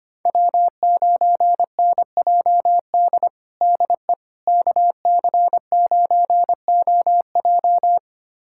Morse code, heard as a keyed tone, about 25 wpm, 700 Hz. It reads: W9NJD DE KC9OJ
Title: USA Callsign DE Callsign - 5 Characters - 25wpm